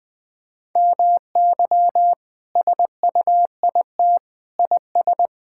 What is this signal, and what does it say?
Morse code, heard as a keyed tone, about 20 wpm, 700 Hz. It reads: MY SUIT IS
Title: Sentences from Top 1600 Words 20wpm